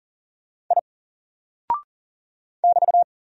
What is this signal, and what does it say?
Morse code, heard as a keyed tone, about 39 wpm, 700 Hz. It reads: I <BT>
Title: Single Character (Letter:Number:Punctuation:Prosign) 40wpm